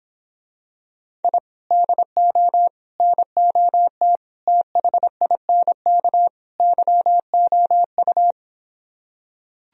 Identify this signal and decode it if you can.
Morse code, heard as a keyed tone, about 26 wpm, 700 Hz. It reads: I DO NOT THINK YOU